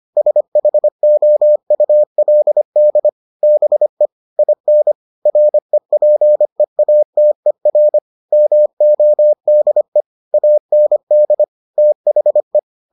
Morse code, heard as a keyed tone, about 25 wpm, 600 Hz. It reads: SHOULD BE IN REPEATER MODE AND THE